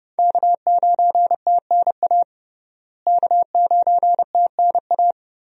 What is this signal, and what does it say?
Morse code, heard as a keyed tone, about 30 wpm, 700 Hz. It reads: K9TNA K9TNA
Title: USA Call Signs - 5 Characters - Repeated 3x - 30wpm